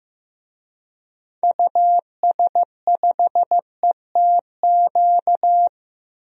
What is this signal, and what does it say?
Morse code, heard as a keyed tone, about 15 wpm, 700 Hz. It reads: US5ETQ
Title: DXCC-100 Call Signs - 6 Characters - Repeated 3x - 15wpm